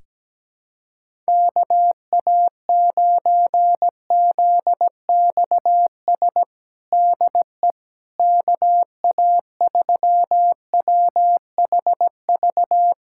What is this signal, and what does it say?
Morse code, heard as a keyed tone, about 17 wpm, 700 Hz. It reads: KA9ZXS DE KA3WHV
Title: USA Callsign DE Callsign - 6 Characters - 17wpm